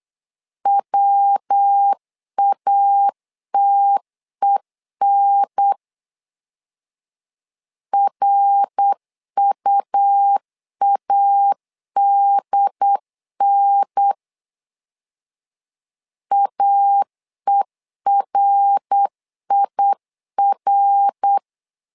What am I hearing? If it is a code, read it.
Morse code, heard as a keyed tone, about 8 wpm, 800 Hz. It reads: WATEN RUADN AERIR
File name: LEZIONI / 3 MWRO / 3 MWR 8wpm 2444